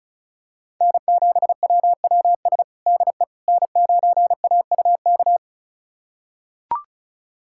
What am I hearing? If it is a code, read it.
Morse code, heard as a keyed tone, about 34 wpm, 700 Hz. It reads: N7WWS DE N9AUK
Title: USA Callsign DE Callsign - 5 Characters - 35wpm